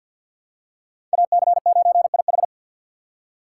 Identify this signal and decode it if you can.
Morse code, heard as a keyed tone, about 47 wpm, 700 Hz. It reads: AC9IH
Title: USA Call Signs - 6x Character Spacing - 50wpm